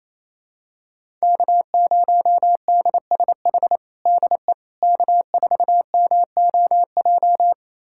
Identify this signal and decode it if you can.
Morse code, heard as a keyed tone, about 27 wpm, 700 Hz. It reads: K0DSH DE K4MOJ
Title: USA Callsign DE Callsign - 5 Characters - 28wpm